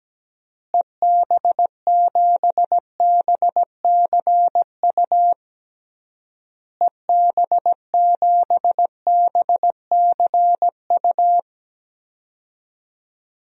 Morse code, heard as a keyed tone, about 17 wpm, 700 Hz. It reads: EB7BCU EB7BCU
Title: DXCC-100 Call Signs - 6 Characters - Repeated 3x - 17wpm